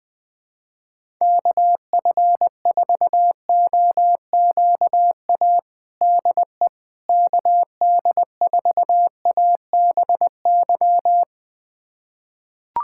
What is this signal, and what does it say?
Morse code, heard as a keyed tone, about 20 wpm, 700 Hz. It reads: KF4OQA DE KD4ABY
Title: USA Callsign DE Callsign 20wpm